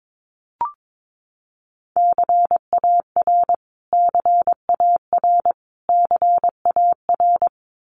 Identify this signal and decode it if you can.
Morse code, heard as a keyed tone, about 22 wpm, 700 Hz. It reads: CAR CAR CAR
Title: Top 300 Words Repeated 22wpm